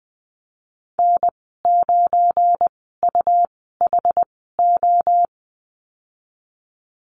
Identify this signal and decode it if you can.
Morse code, heard as a keyed tone, about 20 wpm, spaced as for 15 wpm, 700 Hz. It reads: N9UHO
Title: 179 Lesson 29g - Slash - Call Signs 20wpm